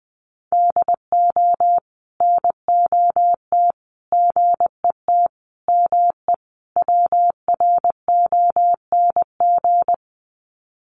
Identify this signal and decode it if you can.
Morse code, heard as a keyed tone, about 20 wpm, 700 Hz. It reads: DO NOT GET ME WRONG
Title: Sets of 5 Words Encore 20wpm